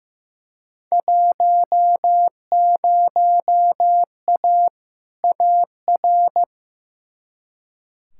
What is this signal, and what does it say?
Morse code, heard as a keyed tone, about 15 wpm, 700 Hz. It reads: 10A AR
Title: Field Day Exchanges Encore - 15wpm